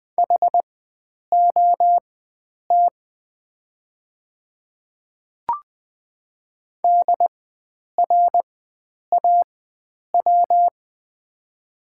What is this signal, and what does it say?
Morse code, heard as a keyed tone, about 20 wpm, spaced as for 9 wpm, 700 Hz. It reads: HOT DRAW
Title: 4x Character Spacing - Top 500 Words - 20wpm